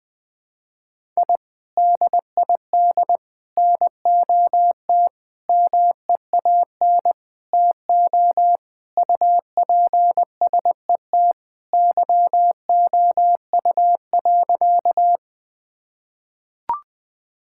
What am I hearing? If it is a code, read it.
Morse code, heard as a keyed tone, about 19 wpm, 700 Hz. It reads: I DID NOT MEAN TO UPSET YOU.
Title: Sentences with 5-Word Phrases - Spoken Beginning - 20wpm